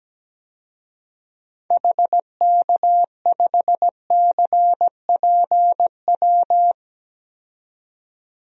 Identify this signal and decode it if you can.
Morse code, heard as a keyed tone, about 17 wpm, 700 Hz. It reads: HK5CPW
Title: DXCC-100 Call Signs - 6 Characters - 17wpm